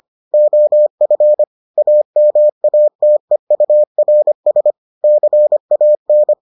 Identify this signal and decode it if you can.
Morse code, heard as a keyed tone, about 25 wpm, 600 Hz. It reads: OF AMATEURS CAN